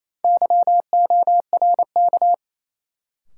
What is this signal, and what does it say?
Morse code, heard as a keyed tone, about 28 wpm, 700 Hz. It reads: YORK
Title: Sets of 2 Words - New Word Drill - 28wpm